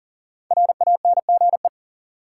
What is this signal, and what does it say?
Morse code, heard as a keyed tone, about 39 wpm, 700 Hz. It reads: RANGE
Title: Five Letter Words - 40wpm